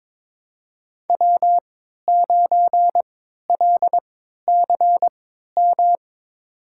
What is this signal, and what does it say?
Morse code, heard as a keyed tone, about 22 wpm, spaced as for 13 wpm, 700 Hz. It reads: W9LCM
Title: USA Call Signs - 5 Characters - 3x Character Spacing - 22wpm